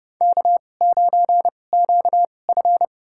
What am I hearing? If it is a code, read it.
Morse code, heard as a keyed tone, about 30 wpm, spaced as for 22 wpm, 700 Hz. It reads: K9QF
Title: USA Call Signs - 4 Characters - 2x Character Spacing - 30wpm